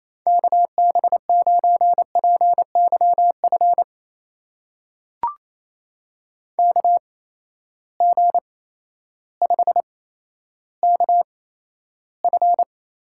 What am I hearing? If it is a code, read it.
Morse code, heard as a keyed tone, about 28 wpm, spaced as for 13 wpm, 700 Hz. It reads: KB9PYF KG5KF
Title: USA Call Signs - 6 Characters - 8x Character Spacing - 28wpm